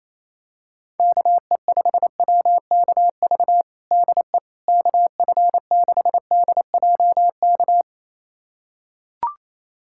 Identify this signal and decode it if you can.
Morse code, heard as a keyed tone, about 28 wpm, 700 Hz. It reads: KE5WKV DE KF6DJK
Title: USA Callsign DE Callsign - 6 Characters - 28wpm